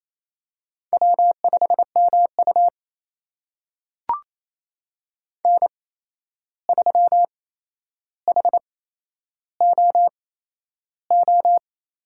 Morse code, heard as a keyed tone, about 28 wpm, spaced as for 11 wpm, 700 Hz. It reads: W5MU N3HOO
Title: USA Call Signs - 8x Character Spacing - 28wpm